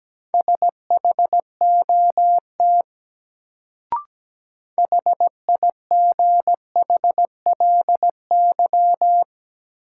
Morse code, heard as a keyed tone, about 17 wpm, 700 Hz. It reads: SHOT HIGHLY
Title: Sentences from Top 1100 Words - New Word Drill - 17wpm